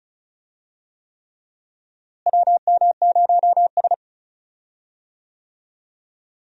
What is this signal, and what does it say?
Morse code, heard as a keyed tone, about 36 wpm, 700 Hz. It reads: WM0S
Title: USA Call Signs - 4 Characters - Repeated 3x - 35wpm